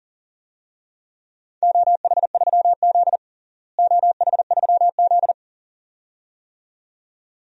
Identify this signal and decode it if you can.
Morse code, heard as a keyed tone, about 40 wpm, 700 Hz. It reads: OH3Z OH3Z
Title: DXCC-100 Call Signs - 4 Characters - Repeated 2x - 40wpm